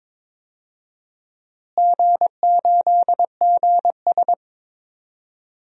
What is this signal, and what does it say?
Morse code, heard as a keyed tone, about 22 wpm, 700 Hz. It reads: G8GS
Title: DXCC-100 Call Signs - 4 Characters - Repeated 3x - 22wpm